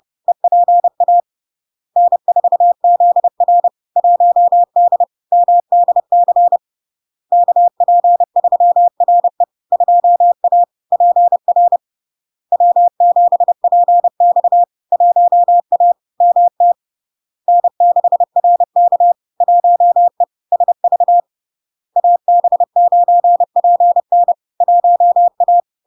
Morse code, heard as a keyed tone, about 30 wpm, 700 Hz. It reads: EPA N4ZR 1D MDC KP3RE 2A PR W7PX 1A MT N6RK 1E SV AB9PN 1A
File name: LARK-FD-2022-Log-in-CW-30wpm-700-Hz